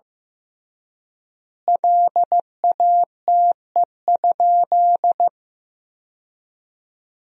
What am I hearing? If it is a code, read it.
Morse code, heard as a keyed tone, about 15 wpm, 700 Hz. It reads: LATE?
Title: 3x Word-Spacing - Sentences from Top 300 Words - 15wpm